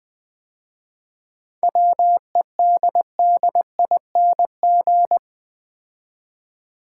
Morse code, heard as a keyed tone, about 20 wpm, 700 Hz. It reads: WEDDING
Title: 7 Letter Words - First 3 Letters Given - 20wpm